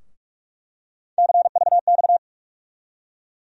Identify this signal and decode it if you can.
Morse code, heard as a keyed tone, about 42 wpm, 700 Hz. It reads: CVX
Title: USA Call Sign Postfixes - 3 Letters 45wpm